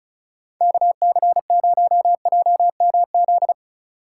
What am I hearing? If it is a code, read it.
Morse code, heard as a keyed tone, about 34 wpm, 700 Hz. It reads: KC0JMZ
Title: USA Call Signs - 4x Character Spacing - 35wpm